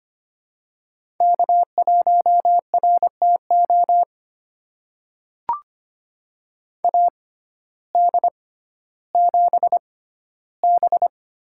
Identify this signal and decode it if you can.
Morse code, heard as a keyed tone, about 25 wpm, spaced as for 14 wpm, 700 Hz. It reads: K1RTO AD7B
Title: USA Call Signs - 5 Characters - 6x Character Spacing - 25wpm